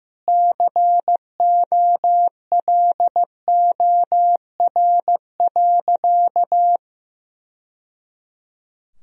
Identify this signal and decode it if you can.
Morse code, heard as a keyed tone, about 15 wpm, 700 Hz. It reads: COLOR.
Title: Sentences from Top 1000 Words - Review Part 2 - 15wpm